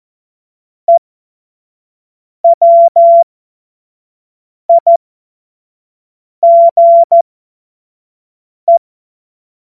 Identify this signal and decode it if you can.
Morse code, heard as a keyed tone, about 14 wpm, spaced as for 5 wpm, 680 Hz. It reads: EWIGE
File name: Morsekurs / Witze / Witz 21. Dezember 2021, 50 F70